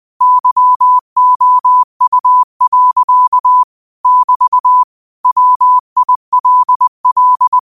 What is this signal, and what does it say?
Morse code, heard as a keyed tone, about 20 wpm, 1000 Hz. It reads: YOU. <BT> WILL